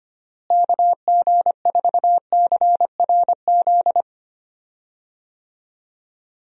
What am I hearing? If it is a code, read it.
Morse code, heard as a keyed tone, about 25 wpm, 700 Hz. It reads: KG4CRZ
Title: USA Call Signs - 6 Characters - Repeated 2x - 25wpm